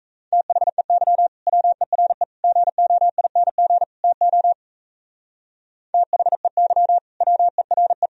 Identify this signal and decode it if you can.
Morse code, heard as a keyed tone, about 34 wpm, 700 Hz. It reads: THEY WERE GOING TO THEY WERE
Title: Speed-Racing - Sets of 4 Words 28wpm